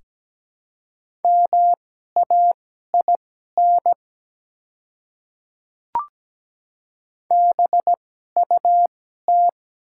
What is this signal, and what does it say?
Morse code, heard as a keyed tone, about 17 wpm, spaced as for 12 wpm, 700 Hz. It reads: MAIN BUT
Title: 2x Character Spacing - Top 300 Words - 17wpm